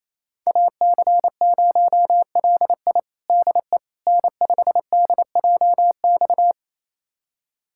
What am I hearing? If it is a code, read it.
Morse code, heard as a keyed tone, about 28 wpm, 700 Hz. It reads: AC0LI DE N5DJX
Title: USA Callsign DE Callsign 28wpm